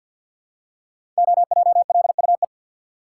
Morse code, heard as a keyed tone, about 52 wpm, 700 Hz. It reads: OJLUE
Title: Code Groups - 5 Letter:Numbers - 50wpm